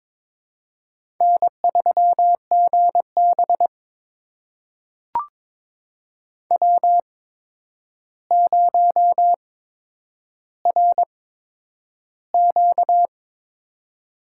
Morse code, heard as a keyed tone, about 22 wpm, spaced as for 9 wpm, 700 Hz. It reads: N3GB W0RQ
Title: USA Call Signs - 4 Characters - 8x Character Spacing - 22wpm